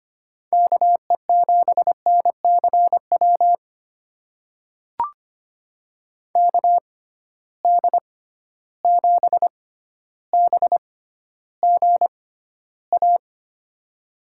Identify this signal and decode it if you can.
Morse code, heard as a keyed tone, about 25 wpm, spaced as for 13 wpm, 700 Hz. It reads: KE7NCW KD7BGA
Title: USA Call Signs - 6 Characters - 6x Character Spacing - 25wpm